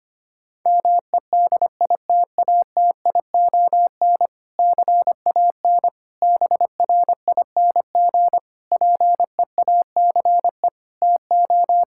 Morse code, heard as a keyed tone, about 25 wpm, 700 Hz. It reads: MEDITATION CAN BRING PEACE TO
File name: practice / 25wpm / Difficult Sentences with Hints - 25wpm